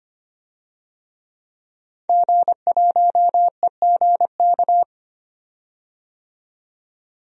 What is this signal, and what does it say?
Morse code, heard as a keyed tone, about 25 wpm, 700 Hz. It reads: G1EGK
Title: DXCC-100 Call Signs - 5 Characters - 25wpm